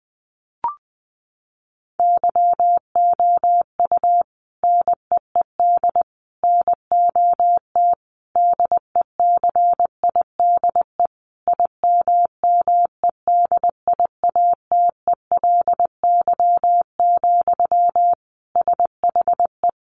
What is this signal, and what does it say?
Morse code, heard as a keyed tone, about 20 wpm, 700 Hz. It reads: YOU NEED NOT DECIDE IMMEDIATELY, SHE
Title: Sentences from Top 1500 Words 20wpm